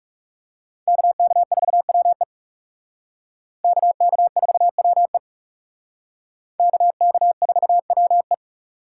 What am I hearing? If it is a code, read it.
Morse code, heard as a keyed tone, about 36 wpm, 700 Hz. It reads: KK4WE KK4WE KK4WE
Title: Speed Racing - USA 2x2 Call Signs 30wpm